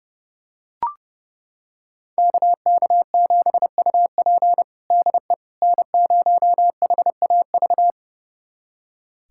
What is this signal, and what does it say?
Morse code, heard as a keyed tone, about 32 wpm, 700 Hz. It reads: KK7UP DE N0HAV
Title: USA Callsign DE Callsign - 5 Characters - 30wpm